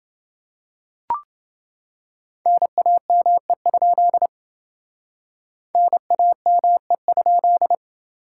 Morse code, heard as a keyed tone, about 28 wpm, 700 Hz. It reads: NAME? NAME?